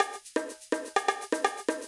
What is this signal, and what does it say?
There is no code.